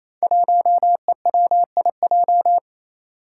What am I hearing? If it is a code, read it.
Morse code, heard as a keyed tone, about 28 wpm, 700 Hz. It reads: 1EWIJ
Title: Five Challenging Letter-Number Combinations 28wpm